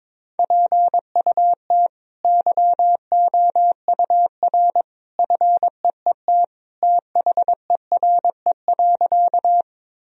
Morse code, heard as a keyed tone, about 22 wpm, 700 Hz. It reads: PUT YOUR FEET THERE.
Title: Sentences from Top 1000 Words - Review Part 1 - 22wpm